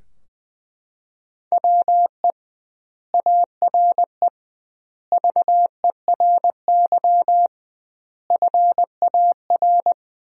Morse code, heard as a keyed tone, about 20 wpm, 700 Hz. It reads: WE ARE VERY FAR
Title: 2x Word-Spacing - Sentences from Top 300 Words - 20wpm